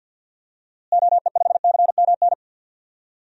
Morse code, heard as a keyed tone, about 52 wpm, 700 Hz. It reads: OE5CMN
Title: DXCC-100 Call Signs - 6 Characters - Repeated 2x - 50wpm